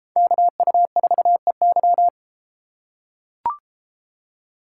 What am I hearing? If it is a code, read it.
Morse code, heard as a keyed tone, about 33 wpm, 700 Hz. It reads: KU4EY